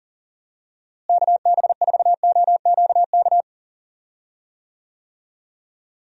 Morse code, heard as a keyed tone, about 40 wpm, 700 Hz. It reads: KB4OQK
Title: USA Call Signs - 6 Characters - Repeated 3x - 40wpm